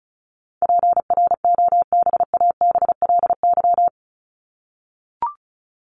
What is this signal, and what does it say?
Morse code, heard as a keyed tone, about 34 wpm, 700 Hz. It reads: PROBABLY
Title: Top 501-750 Words 35wpm